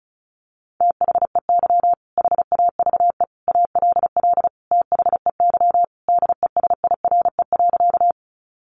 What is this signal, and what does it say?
Morse code, heard as a keyed tone, about 34 wpm, 700 Hz. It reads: THEY HAVE ALL THEY DESIRE.
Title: Sentences from Top 1100 Words 35wpm